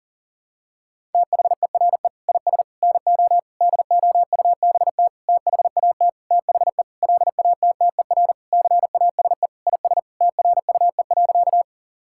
Morse code, heard as a keyed tone, about 40 wpm, 700 Hz. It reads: THERE IS NO DOUBT THAT THE LATTER CASE IS TRUE.
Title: Sentences from Top 1600 Words 40wpm